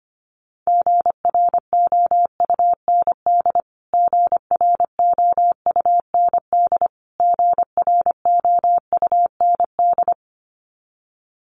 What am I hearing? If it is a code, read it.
Morse code, heard as a keyed tone, about 25 wpm, 700 Hz. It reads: GROUND GROUND GROUND
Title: Top 500 Words Repeated 25wpm